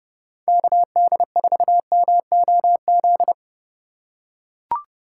Morse code, heard as a keyed tone, about 31 wpm, 700 Hz. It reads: KD4MOZ
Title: USA Call Signs - 6 Characters - 4x Character Spacing - 30wpm